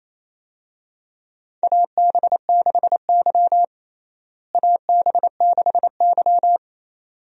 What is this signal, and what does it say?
Morse code, heard as a keyed tone, about 27 wpm, 700 Hz. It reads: AB6Y AB6Y
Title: USA Call Signs - 4 Characters - Repeated 2x - 28wpm